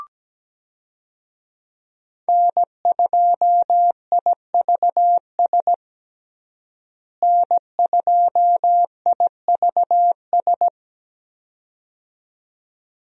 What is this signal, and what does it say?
Morse code, heard as a keyed tone, about 17 wpm, 700 Hz. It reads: N2IVS N2IVS
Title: USA Call Signs - 5 Characters - Repeated 2x - 17wpm